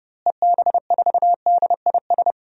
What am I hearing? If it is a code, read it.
Morse code, heard as a keyed tone, about 32 wpm, 700 Hz. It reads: EB4DIS
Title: DXCC-100 Call Signs - 6 Characters - Repeated 3x - 30wpm